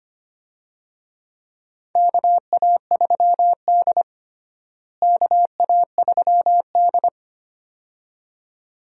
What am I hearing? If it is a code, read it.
Morse code, heard as a keyed tone, about 25 wpm, 700 Hz. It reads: KA3D KA3D
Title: USA Call Signs - 4 Characters - Repeated 2x - 25wpm